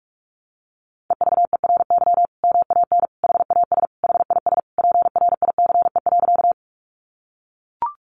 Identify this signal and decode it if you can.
Morse code, heard as a keyed tone, about 44 wpm, 700 Hz. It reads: EVERY MAN HAS HIS PRICE.
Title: Sentences from Top 600 Words 45wpm